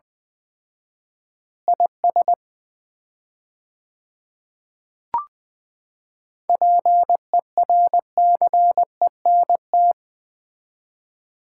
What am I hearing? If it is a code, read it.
Morse code, heard as a keyed tone, about 20 wpm, 700 Hz. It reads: IS PERCENT
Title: 4x Word-Spacing - Sets of 3 Words - 20wpm